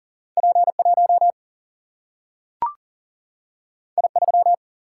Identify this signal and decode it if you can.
Morse code, heard as a keyed tone, about 39 wpm, 700 Hz. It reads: P 1 I 3
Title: Single-Letter Single-Number 40wpm